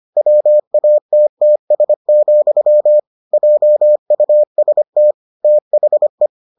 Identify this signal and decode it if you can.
Morse code, heard as a keyed tone, about 25 wpm, 600 Hz. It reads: WATTS, JUST THE